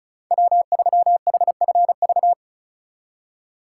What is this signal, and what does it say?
Morse code, heard as a keyed tone, about 34 wpm, 700 Hz. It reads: W3HFV
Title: USA Call Signs - 5 Characters - 3x Character Spacing - 35wpm